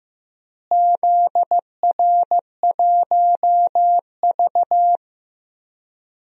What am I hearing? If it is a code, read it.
Morse code, heard as a keyed tone, about 15 wpm, 700 Hz. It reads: ZR1V
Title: DXCC-100 Call Signs - 4 Characters - Repeated 3x - 15wpm